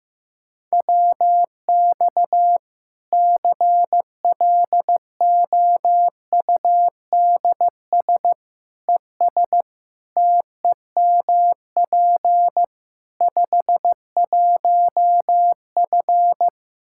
Morse code, heard as a keyed tone, about 15 wpm, 700 Hz. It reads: WX CLOUDS ES TEMP 51F
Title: Practice QSO - Exchange 2 - Info - 15wpm